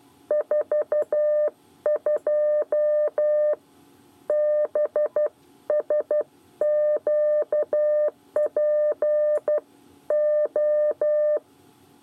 Morse code, heard as a keyed tone, about 12 wpm, 600 Hz. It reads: 42 BSQPO
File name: 8 bis lezione cw 1